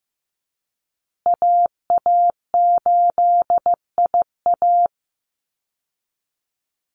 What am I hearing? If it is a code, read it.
Morse code, heard as a keyed tone, about 15 wpm, 700 Hz. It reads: AA8IA
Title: Call Signs Repeated 2x - 15wpm